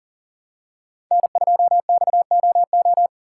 Morse code, heard as a keyed tone, about 40 wpm, 700 Hz. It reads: N2XOO
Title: USA Call Signs - 5 Characters - 4x Character Spacing - 40wpm